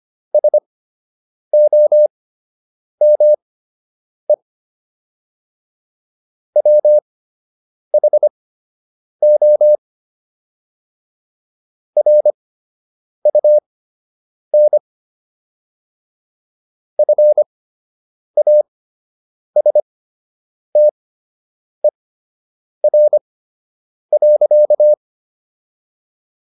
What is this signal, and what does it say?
Morse code, heard as a keyed tone, about 25 wpm, spaced as for 8 wpm, 600 Hz. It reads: SOME WHO RUN FASTER.